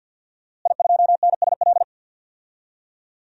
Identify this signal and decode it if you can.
Morse code, heard as a keyed tone, about 47 wpm, 700 Hz. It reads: I2NSL
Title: DXCC-100 Call Signs - 5 Characters - Repeated 2x - 50wpm